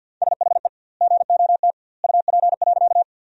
Morse code, heard as a keyed tone, about 51 wpm, 700 Hz. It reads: SHE GOT UP.
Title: 2x Word-Spacing - Sentences from Top 300 Words - 50wpm